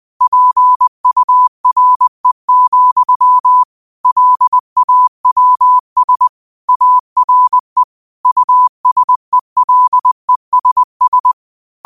Morse code, heard as a keyed tone, about 20 wpm, 1000 Hz. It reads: PURE, LAWS ARE USELESS